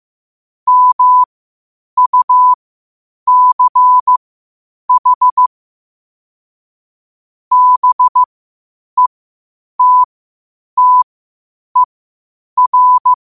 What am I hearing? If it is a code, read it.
Morse code, heard as a keyed tone, about 15 wpm, spaced as for 8 wpm, 1000 Hz. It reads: MUCH BETTER
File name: qotdfiles / 8wpm